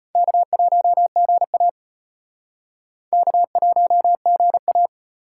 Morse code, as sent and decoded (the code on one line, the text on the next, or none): -.- .---- --. .- / -.- .---- --. .-
K1GA K1GA